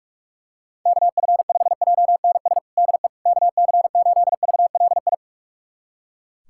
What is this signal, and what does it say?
Morse code, heard as a keyed tone, about 42 wpm, 700 Hz. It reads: KF5JNS DE KC8VLI
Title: USA Callsign DE Callsign - 6 Characters - 45wpm